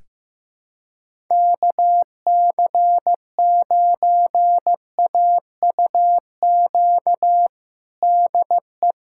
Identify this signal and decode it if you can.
Morse code, heard as a keyed tone, about 15 wpm, 700 Hz. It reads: KC9AUQ DE
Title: USA Callsign DE Callsign - 6 Characters - 15wpm